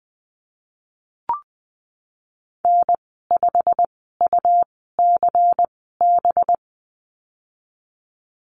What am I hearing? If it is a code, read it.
Morse code, heard as a keyed tone, about 20 wpm, spaced as for 15 wpm, 700 Hz. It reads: N5UCB
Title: 193 Lesson 31g - Letter v - Call Signs 20wpm